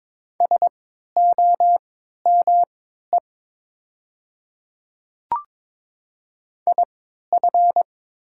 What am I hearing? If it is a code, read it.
Morse code, heard as a keyed tone, about 22 wpm, spaced as for 12 wpm, 700 Hz. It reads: SOME IF
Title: 3x Character Spacing - Top 100 Words - 22wpm